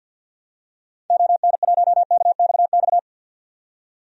Morse code, heard as a keyed tone, about 47 wpm, 700 Hz. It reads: ON1KXX
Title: DXCC-100 Call Signs - 6 Characters - 50wpm